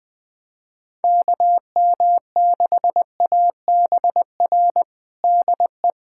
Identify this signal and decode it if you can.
Morse code, heard as a keyed tone, about 20 wpm, 700 Hz. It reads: KM6ABR DE
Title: USA Callsign DE Callsign 20wpm